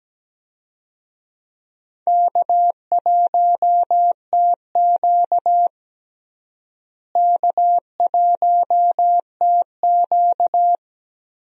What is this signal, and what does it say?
Morse code, heard as a keyed tone, about 17 wpm, 700 Hz. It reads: K1TQ K1TQ
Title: USA Call Signs - 4 Characters - Repeated 2x - 17wpm